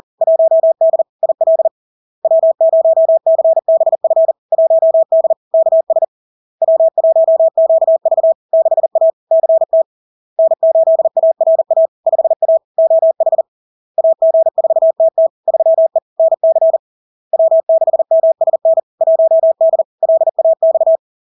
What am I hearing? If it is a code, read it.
Morse code, heard as a keyed tone, about 40 wpm, 650 Hz. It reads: 1D IL W0CBF 1D KS W1QV 6A CT N8ARA 5A OH AG4TT 3E NC W6MSN 1D LAX
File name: LARK-FD-2022-Log-in-CW-40wpm-650-Hz